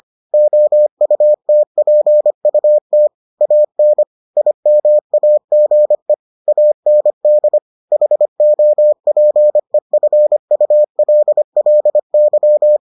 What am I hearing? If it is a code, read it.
Morse code, heard as a keyed tone, about 25 wpm, 600 Hz. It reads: OUTPUT AN IMAGE AND HOPEFULLY